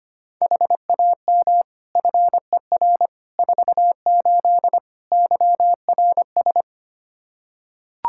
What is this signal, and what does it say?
Morse code, heard as a keyed tone, about 25 wpm, 700 Hz. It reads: HAM FER 48 YRS
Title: Practice QSO - Exchange 3 - More Info - 25wpm